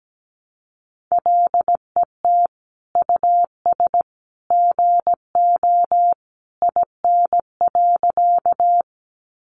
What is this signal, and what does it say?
Morse code, heard as a keyed tone, about 17 wpm, 700 Hz. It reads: LET US GO IN.
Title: Sentences from Top 300 Words 17wpm